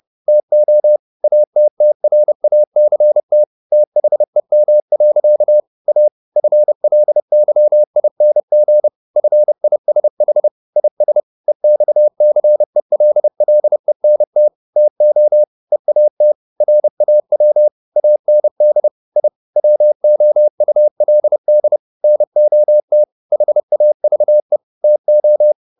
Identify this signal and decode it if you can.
Morse code, heard as a keyed tone, about 30 wpm, 600 Hz. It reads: TO ATTRACT THEM. A FLYING FISH IS EXCELLENT TO EAT RAW AND I WOULD NOT HAVE TO
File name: the old man and the sea / the old man and the sea 1 30wpm